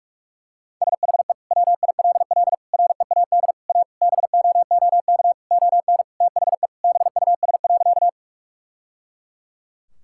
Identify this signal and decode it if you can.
Morse code, heard as a keyed tone, about 41 wpm, 700 Hz. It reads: SHE WILL READ A BOOK ON THE BUS.
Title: Sentences from Top 1000 Words - Review Part 6 - 45wpm